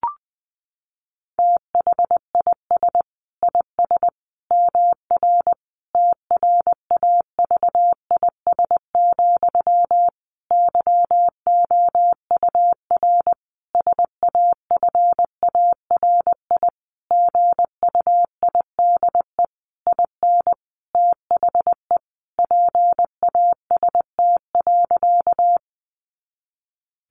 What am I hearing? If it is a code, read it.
Morse code, heard as a keyed tone, about 20 wpm, 700 Hz. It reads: THIS IS MR TRAVIS, YOUR SAFARI GUIDE IN THE PAST.
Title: A Sound Of Thunder 20wpm